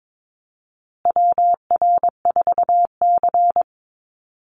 Morse code, heard as a keyed tone, about 22 wpm, 700 Hz. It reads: WR4C
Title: USA Call Signs Repeated 3x 22wpm